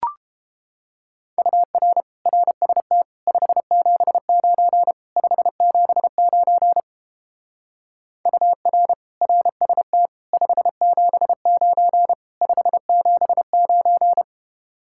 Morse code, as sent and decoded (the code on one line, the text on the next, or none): ..- .-. / .-. ... - / ..... --... ----. / ..... --... ----. / ..- .-. / .-. ... - / ..... --... ----. / ..... --... ----.
UR RST 579 579 UR RST 579 579